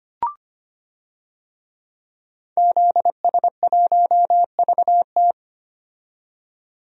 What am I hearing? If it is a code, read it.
Morse code, heard as a keyed tone, about 25 wpm, 700 Hz. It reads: ZS1VT
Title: DXCC-100 Call Signs - 5 Characters - 25wpm